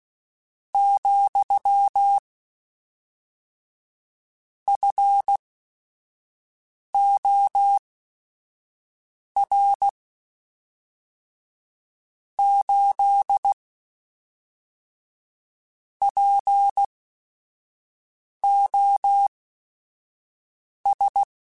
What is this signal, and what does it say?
Morse code, heard as a keyed tone, about 16 wpm, spaced as for 5 wpm, 800 Hz. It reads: , FOR 8 POS
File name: Technical Subjects / Technical Subjects-mp3s / AND Gate @5